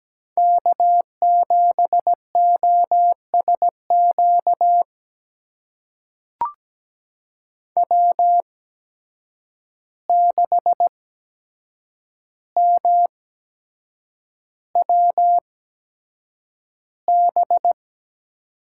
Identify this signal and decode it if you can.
Morse code, heard as a keyed tone, about 17 wpm, spaced as for 7 wpm, 700 Hz. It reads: K7OSQ W6MWB
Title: USA Call Signs - 5 Characters - 8x Character Spacing - 17wpm